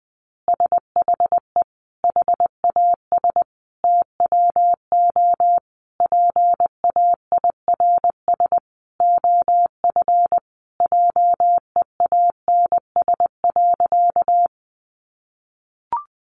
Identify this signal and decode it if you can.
Morse code, heard as a keyed tone, about 20 wpm, 700 Hz. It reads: SHE HAS TWO PAIRS OF JEANS.
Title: Additional Easy Phrases 20wpm